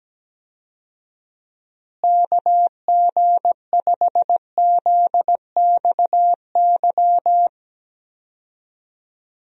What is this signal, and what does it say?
Morse code, heard as a keyed tone, about 17 wpm, 700 Hz. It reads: KG5ZXY
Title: USA Call Signs - 6 Characters - Repeated 2x - 17wpm